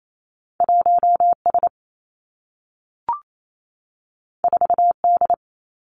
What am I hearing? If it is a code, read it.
Morse code, heard as a keyed tone, about 28 wpm, 700 Hz. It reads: 1 S 4 D
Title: Two Challenging Letter-Number Combinations 28wpm